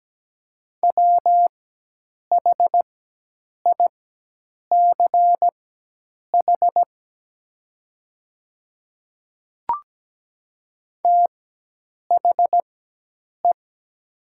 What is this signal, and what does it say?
Morse code, heard as a keyed tone, about 17 wpm, spaced as for 8 wpm, 700 Hz. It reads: WHICH THE
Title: 4x Character Spacing - Top 100 Words - 17wpm